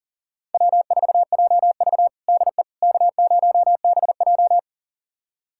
Morse code, heard as a keyed tone, about 40 wpm, 700 Hz. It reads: W4JV DE K0BJ
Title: USA Callsign DE Callsign - 4 Characters - 40wpm